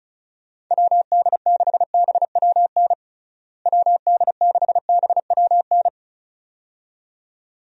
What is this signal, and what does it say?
Morse code, heard as a keyed tone, about 36 wpm, 700 Hz. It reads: WD6BWN WD6BWN
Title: USA Call Signs - 6 Characters - Repeated 2x - 35wpm